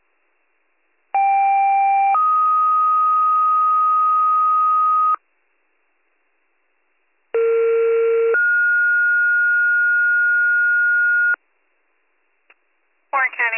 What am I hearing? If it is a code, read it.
Two-tone page, heard as a one-second tone, then a three-second tone, about 790 then 1250 Hz, about 460 then 1430 Hz, with speech. It is heard from 1.1 to 11.4 s.